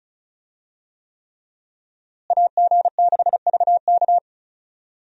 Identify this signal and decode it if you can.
Morse code, heard as a keyed tone, about 34 wpm, 700 Hz. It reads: AG6VK
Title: USA Call Signs - 5 Characters - Repeated 3x - 35wpm